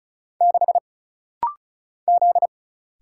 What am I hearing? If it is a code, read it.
Morse code, heard as a keyed tone, about 34 wpm, 700 Hz. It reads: 6 Z
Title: Single Letter-Number - Rapid-Fire 35wpm